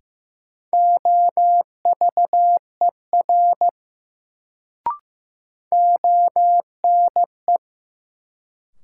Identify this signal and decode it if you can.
Morse code, heard as a keyed tone, about 15 wpm, 700 Hz. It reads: OVER ONE
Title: Sentences from Top 100 Words - New Word Drill - 15wpm